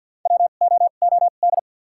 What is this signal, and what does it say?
Morse code, heard as a keyed tone, about 53 wpm, spaced as for 37 wpm, 700 Hz. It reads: WOOD
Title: 2x Character Spacing - Top 300 Words - 50wpm